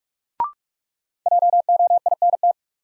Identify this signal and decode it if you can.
Morse code, heard as a keyed tone, about 45 wpm, 700 Hz. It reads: JOINT
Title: Sentences from Top 1300 Words - New Word Drill Alt - 45wpm